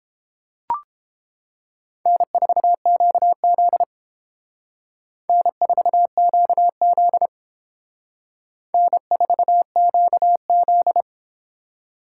Morse code, heard as a keyed tone, about 29 wpm, 700 Hz. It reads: N4QZ N4QZ N4QZ